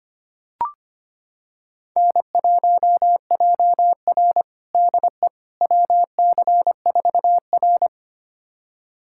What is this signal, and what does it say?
Morse code, heard as a keyed tone, about 25 wpm, 700 Hz. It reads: N1JR DE WC4R
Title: USA Callsign DE Callsign - 4 Characters - 25wpm